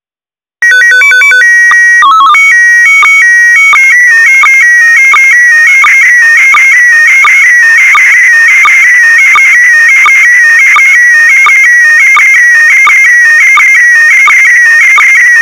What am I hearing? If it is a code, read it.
SSTV (slow-scan television), PD 90.
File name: e9e61cbe-6188-4f30-833c / SSTV OVER FM IMAGE THREE UNDR NET